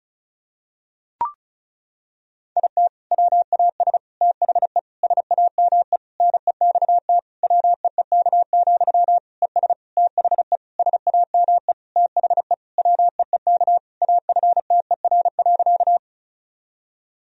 Morse code, heard as a keyed tone, about 36 wpm, 700 Hz. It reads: IT WAS THE SAME NEXT WEEK, ES THE SAME THE WEEK AFTER.
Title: Sentences from Top 500 Words and Abbreviations 35wpm